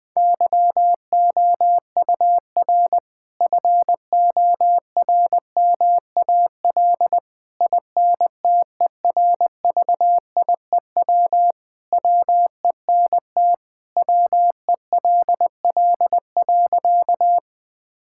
Morse code, heard as a keyed tone, about 21 wpm, 700 Hz. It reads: YOUR FORMAL INTERVIEW WENT WELL.
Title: Sentences from Top 1600 Words 20wpm